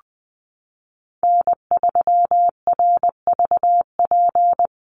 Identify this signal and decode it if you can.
Morse code, heard as a keyed tone, about 20 wpm, 700 Hz. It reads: N3RVP
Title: USA Call Signs Repeated 2x 20wpm